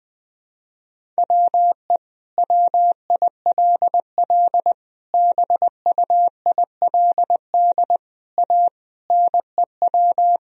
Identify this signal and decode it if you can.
Morse code, heard as a keyed tone, about 20 wpm, 700 Hz. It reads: WE WILL BUILD A NEW
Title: Sentences from Top 1000 Words - Review Part 3 - 20wpm